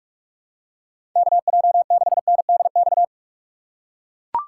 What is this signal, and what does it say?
Morse code, heard as a keyed tone, about 43 wpm, 700 Hz. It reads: KJ6NDX
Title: USA Call Signs - 6x Character Spacing - 45wpm